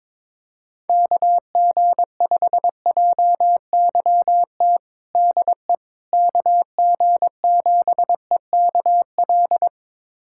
Morse code, heard as a keyed tone, about 22 wpm, 700 Hz. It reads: KG5JYT DE KG7EKL
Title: USA Callsign DE Callsign 22wpm